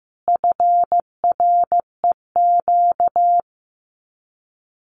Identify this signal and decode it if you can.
Morse code, heard as a keyed tone, about 15 wpm, 700 Hz. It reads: FREQ
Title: Abbreviations Repeated 3x - 15wpm